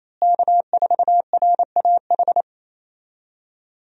Morse code, heard as a keyed tone, about 28 wpm, 700 Hz. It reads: K4RAH
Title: USA Call Signs - 5 Characters - Repeated 3x - 28wpm